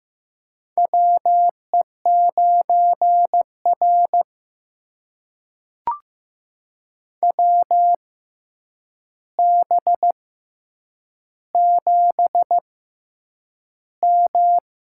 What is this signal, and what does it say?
Morse code, heard as a keyed tone, about 15 wpm, spaced as for 8 wpm, 700 Hz. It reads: WE9R WB7M
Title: USA Call Signs - 4 Characters - 6x Character Spacing - 15wpm